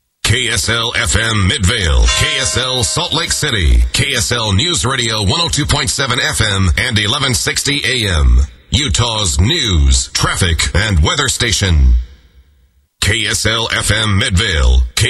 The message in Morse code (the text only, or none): none